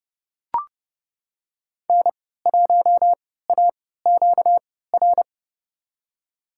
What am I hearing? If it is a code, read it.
Morse code, heard as a keyed tone, about 30 wpm, spaced as for 17 wpm, 700 Hz. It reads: N1AQR
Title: USA Call Signs - 5 Characters - 3x Character Spacing - 30wpm